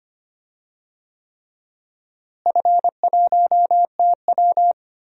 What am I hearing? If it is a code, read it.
Morse code, heard as a keyed tone, about 25 wpm, 700 Hz. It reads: F1TW
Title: DXCC-100 Call Signs - 4 Characters - Repeated 3x - 25wpm